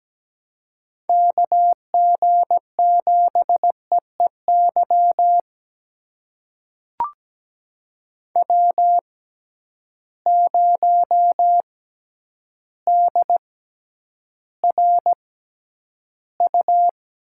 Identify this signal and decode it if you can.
Morse code, heard as a keyed tone, about 17 wpm, spaced as for 9 wpm, 700 Hz. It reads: KG7EEY W0DRU
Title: USA Call Signs - 6x Character Spacing - 17wpm